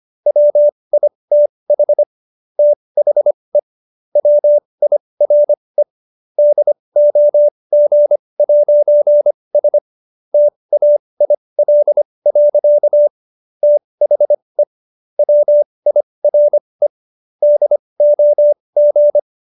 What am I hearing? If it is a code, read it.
Morse code, heard as a keyed tone, about 25 wpm, spaced as for 20 wpm, 590 Hz. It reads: WITH THE WIRE DOG'S TAIL. THE WIRE DOG